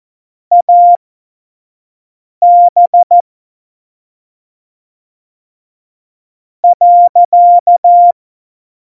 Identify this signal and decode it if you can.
Morse code, heard as a keyed tone, about 14 wpm, spaced as for 3 wpm, 700 Hz. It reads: AB.
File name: Morsekurs / Witze / Witz 22. Januar 2019, 25 F70